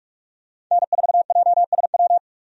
Morse code, heard as a keyed tone, about 44 wpm, 700 Hz. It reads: N4JSW
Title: USA Call Signs - 5 Characters - 45wpm